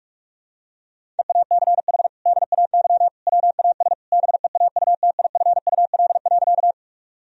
Morse code, heard as a keyed tone, about 41 wpm, 700 Hz. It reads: EACH DAY WAS BEAUTIFUL.
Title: Sentences from Top 1000 Words - Review Part 1 - 45wpm